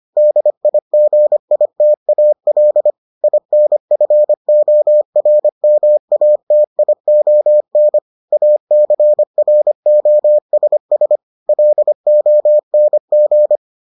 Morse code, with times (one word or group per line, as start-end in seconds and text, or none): 0.2-2.9 DIGITAL
3.2-8.0 INFORMATION
8.3-11.2 ACROSS
11.5-13.6 LONG